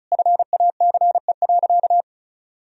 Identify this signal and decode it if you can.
Morse code, heard as a keyed tone, about 36 wpm, 700 Hz. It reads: FACE.
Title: Sentences from Top 1000 Words - Review Part 2 - 35wpm